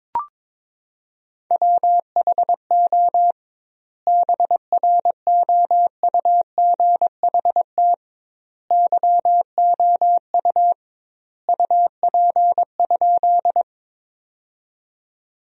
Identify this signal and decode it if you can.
Morse code, heard as a keyed tone, about 22 wpm, 700 Hz. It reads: WHO BROUGHT YOU UP?
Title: 2x Word-Spacing - Sentences from Top 500 Words - 22wpm